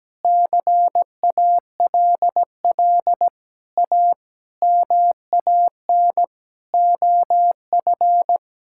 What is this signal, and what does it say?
Morse code, heard as a keyed tone, about 17 wpm, 700 Hz. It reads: CALL A MAN OF
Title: Sentences from Top 1600 Words 17wpm